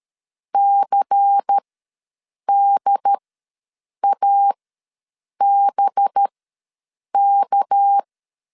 Morse code, heard as a keyed tone, about 13 wpm, spaced as for 7 wpm, 800 Hz. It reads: CDABK
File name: LEZIONI / 9 2 ws 12wpm 3406